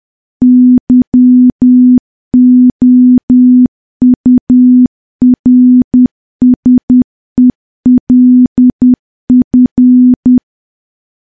Morse code, heard as a keyed tone, about 10 wpm, 250 Hz. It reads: YOURSELF
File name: morse / morse-training-10WPM